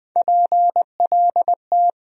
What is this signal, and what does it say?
Morse code, heard as a keyed tone, about 20 wpm, 700 Hz. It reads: PLT
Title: USA Call Sign Postfixes - 3 Letters 20wpm